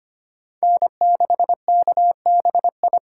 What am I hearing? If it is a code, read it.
Morse code, heard as a keyed tone, about 32 wpm, 700 Hz. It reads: N6KBI